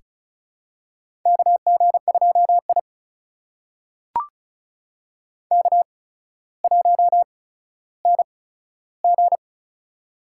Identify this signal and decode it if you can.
Morse code, heard as a keyed tone, about 36 wpm, spaced as for 15 wpm, 700 Hz. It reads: KG2I K1NG
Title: USA Call Signs - 4 Characters - 8x Character Spacing - 35wpm